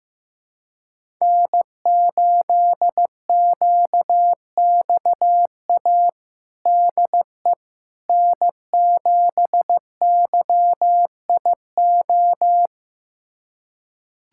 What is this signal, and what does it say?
Morse code, heard as a keyed tone, about 15 wpm, 700 Hz. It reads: N8QXA DE N7YIO
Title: USA Callsign DE Callsign - 5 Characters - 15wpm